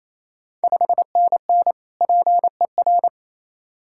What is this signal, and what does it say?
Morse code, heard as a keyed tone, about 28 wpm, 700 Hz. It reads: 5NN PER
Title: Ontario QSO Party Exchange 28wpm